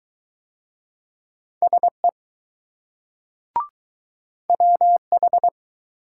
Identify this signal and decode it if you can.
Morse code, heard as a keyed tone, about 20 wpm, 700 Hz. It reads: SE WH